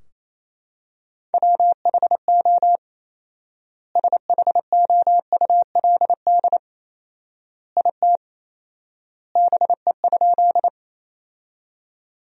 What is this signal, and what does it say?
Morse code, heard as a keyed tone, about 28 wpm, 700 Hz. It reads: WHO SHOULD IT BE?
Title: 4x Word-Spacing - Sentences from Top 300 Words - 28wpm